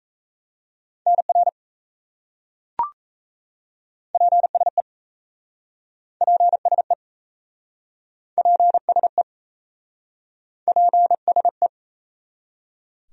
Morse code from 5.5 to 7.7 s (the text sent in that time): PSE